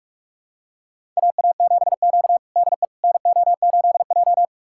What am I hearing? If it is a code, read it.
Morse code, heard as a keyed tone, about 46 wpm, 700 Hz. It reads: AA7Q DE NO8J
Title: USA Callsign DE Callsign - 4 Characters - 45wpm